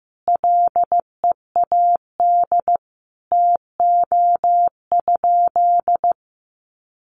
Morse code, heard as a keyed tone, about 15 wpm, 700 Hz. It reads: LEAD TO?
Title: Sentences from Top 1400 Words 15wpm